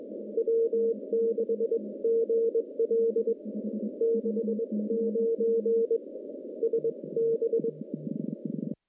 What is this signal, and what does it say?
Morse code, heard as a keyed tone, about 22 wpm, spaced as for 19 wpm, 460 Hz. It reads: W6GL 69 SB
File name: contests / CW-SS-k5tr- -1300